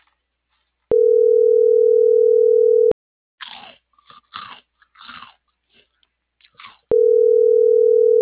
Ringback tone (North America), 440 + 480 Hz.